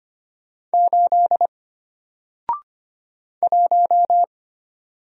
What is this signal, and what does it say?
Morse code, heard as a keyed tone, about 25 wpm, 700 Hz. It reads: 8 1
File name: Single-Letter-Number-25wpm